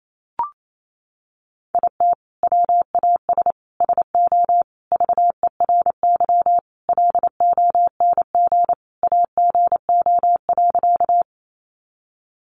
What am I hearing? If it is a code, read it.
Morse code, heard as a keyed tone, about 28 wpm, 700 Hz. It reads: IT WAS SO VERY LONG AGO.
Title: Sentences from Top 500 Words 28wpm